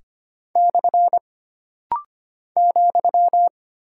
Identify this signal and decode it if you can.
Morse code, heard as a keyed tone, about 25 wpm, 700 Hz. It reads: / ,
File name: Single Punctuation-Prosign - Rapid-Fire - 25wpm